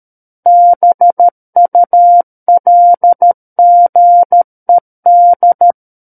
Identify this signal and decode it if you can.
Morse code, heard as a keyed tone, about 13 wpm, 700 Hz. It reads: BULGED
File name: Book One / 04-Chapter 4 - 13 WPM